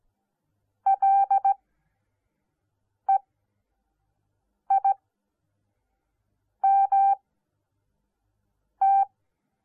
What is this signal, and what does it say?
Morse code, heard as a keyed tone, about 17 wpm, spaced as for 5 wpm, 780 Hz. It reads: LEIMT